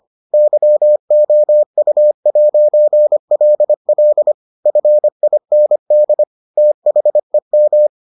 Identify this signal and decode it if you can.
Morse code, heard as a keyed tone, about 25 wpm, 600 Hz. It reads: YOU'LL FIND THEM